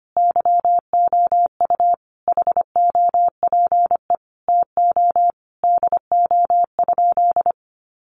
Morse code, heard as a keyed tone, about 25 wpm, 700 Hz. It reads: YOU HOPE TO DO?
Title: Sentences from Top 1000 Words 25wpm